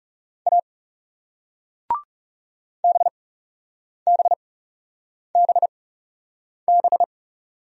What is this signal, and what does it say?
Morse code, heard as a keyed tone, about 41 wpm, spaced as for 8 wpm, 700 Hz. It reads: A BBBB